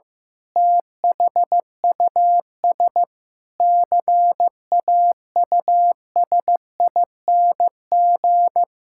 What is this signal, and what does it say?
Morse code, heard as a keyed tone, about 15 wpm, 700 Hz. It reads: THUS CAUSING
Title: Pangram Challenge v2 - 15wpm